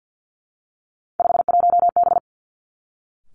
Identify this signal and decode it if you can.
Morse code, heard as a keyed tone, about 50 wpm, 700 Hz. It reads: 51B
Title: Three Challenging Letter-Number Combinations 50wpm